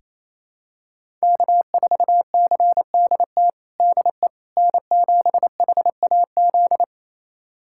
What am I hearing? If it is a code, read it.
Morse code, heard as a keyed tone, about 27 wpm, 700 Hz. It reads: K4CDT DE N7HAZ
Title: USA Callsign DE Callsign - 5 Characters - 28wpm